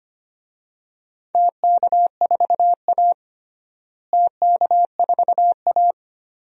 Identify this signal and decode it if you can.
Morse code, heard as a keyed tone, about 25 wpm, 700 Hz. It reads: TK4A TK4A
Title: DXCC-100 Call Signs - 4 Characters - Repeated 2x - 25wpm